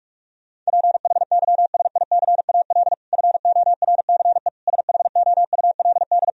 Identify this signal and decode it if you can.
Morse code, heard as a keyed tone, about 42 wpm, 700 Hz. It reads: PHYSICAL FORCE SHOULD
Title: Sentences from Top 1000 Words - Review Part 4 - 45wpm